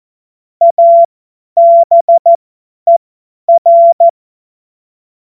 Morse code, heard as a keyed tone, about 14 wpm, spaced as for 10 wpm, 680 Hz. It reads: ABER
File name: Morsekurs / Witze / Witz 19 März 2024 50 F70